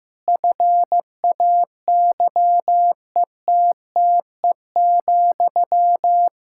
Morse code, heard as a keyed tone, about 15 wpm, 700 Hz. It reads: FAYETTE,
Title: Practice QSO - Exchange 1 - Beginning - 15wpm